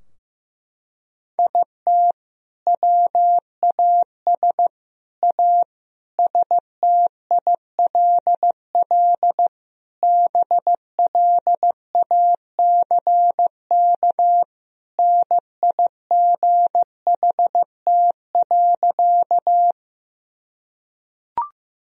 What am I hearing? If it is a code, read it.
Morse code, heard as a keyed tone, about 15 wpm, 700 Hz. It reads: IT WAS A STILL BLACK NIGHT.
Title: Sentences from Top 1000 Words - Review Part 2 - 15wpm